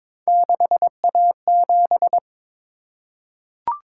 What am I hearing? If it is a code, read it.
Morse code, heard as a keyed tone, about 23 wpm, 700 Hz. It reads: 6A7
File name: practice / 22wpm / Three Letter-Number Combinations - 22wpm